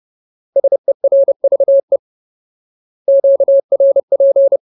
Morse code, heard as a keyed tone, about 30 wpm, 550 Hz. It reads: SERVE QRP
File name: Ultimate Word List / 30WPM Ultimate